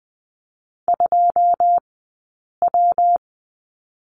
Morse code, heard as a keyed tone, about 20 wpm, spaced as for 8 wpm, 700 Hz. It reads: 2W
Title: 126 Lesson 21d - Number 6 - Two Characters 20wpm